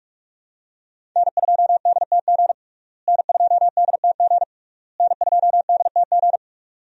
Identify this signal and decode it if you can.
Morse code, heard as a keyed tone, about 46 wpm, 700 Hz. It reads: N2DTG N2DTG N2DTG
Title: USA Call Signs - 5 Characters - Repeated 3x - 45wpm